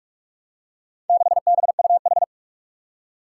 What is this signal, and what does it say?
Morse code, heard as a keyed tone, about 41 wpm, 700 Hz. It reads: 6BUH
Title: Four Challenging Letter-Number Combinations 45wpm